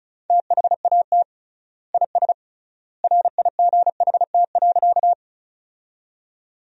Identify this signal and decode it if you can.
Morse code, heard as a keyed tone, about 34 wpm, 700 Hz. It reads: THAT IS RIGHT.
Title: 3x Word-Spacing - Sentences from Top 200 Words - 35wpm